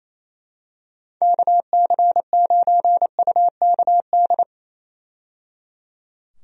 Morse code, heard as a keyed tone, about 28 wpm, 700 Hz. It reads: KC9UKD
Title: USA Call Signs - 6 Characters - 28wpm